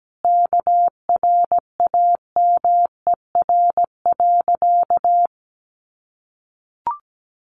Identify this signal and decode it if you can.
Morse code, heard as a keyed tone, about 17 wpm, 700 Hz. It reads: KRAMER.
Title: A Sound Of Thunder 17wpm